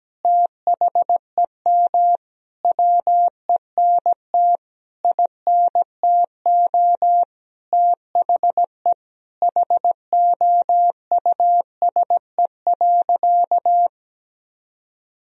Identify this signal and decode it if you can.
Morse code, heard as a keyed tone, about 17 wpm, 700 Hz. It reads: THEM WENT INTO THE HOUSE.
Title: Sentences from Top 1000 Words - Review Part 1 - 17wpm